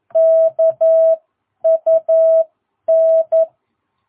Morse code, heard as a keyed tone, about 11 wpm, spaced as for 9 wpm, 660 Hz. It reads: KUN
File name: jh8cbh / 2022cw